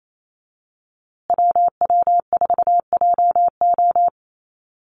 Morse code, heard as a keyed tone, about 28 wpm, 700 Hz. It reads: WW4JO
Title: USA Call Signs Repeated 2x 28wpm